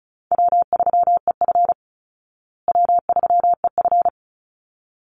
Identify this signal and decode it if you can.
Morse code, heard as a keyed tone, about 34 wpm, 700 Hz. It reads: W3EF W3EF
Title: Easy Call Signs Repeated 2x - 35wpm